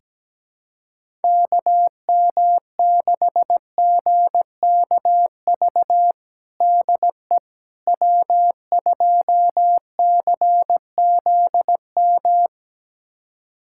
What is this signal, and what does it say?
Morse code, heard as a keyed tone, about 17 wpm, 700 Hz. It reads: KM6GKV DE W2CZM
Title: USA Callsign DE Callsign 17wpm